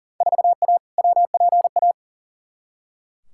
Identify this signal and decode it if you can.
Morse code, heard as a keyed tone, about 37 wpm, 700 Hz. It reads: 4A WPA
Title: Field Day Exchanges Encore - 40wpm